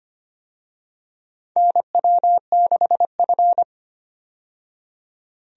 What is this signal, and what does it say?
Morse code, heard as a keyed tone, about 25 wpm, 700 Hz. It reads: NW6F
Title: USA Call Signs - 4 Characters - 25wpm